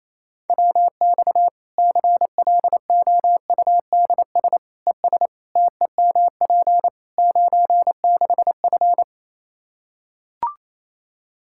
Morse code, heard as a keyed tone, about 28 wpm, 700 Hz. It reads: WX CLOUDS ES TEMP 96F
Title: Practice QSO - Exchange 2 - Info - 28wpm